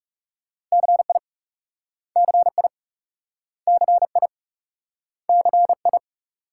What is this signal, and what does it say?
Morse code, heard as a keyed tone, about 38 wpm, 700 Hz. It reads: CI CI CI CI